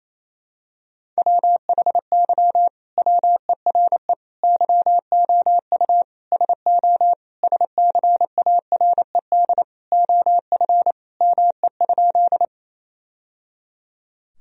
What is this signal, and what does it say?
Morse code, heard as a keyed tone, about 28 wpm, 700 Hz. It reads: WHY WERE YOU SO SCARED OF ME?
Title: Sentences from Top 1000 Words - Review Part 4 - 28wpm